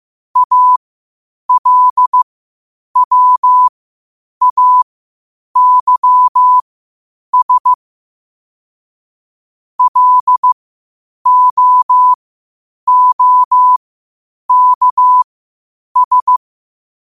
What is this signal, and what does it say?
Morse code, heard as a keyed tone, about 15 wpm, spaced as for 8 wpm, 1000 Hz. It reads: ALWAYS LOOKS
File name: qotdfiles / 8wpm